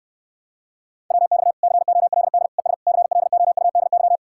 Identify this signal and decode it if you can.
Morse code, heard as a keyed tone, about 69 wpm, 700 Hz. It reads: WX COLD ES CLOUDY